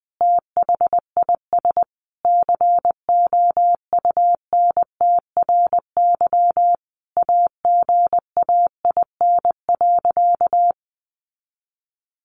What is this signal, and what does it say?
Morse code, heard as a keyed tone, about 20 wpm, 700 Hz. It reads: THIS COUNTRY AGAIN.
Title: Sentences from Top 1200 Words 20wpm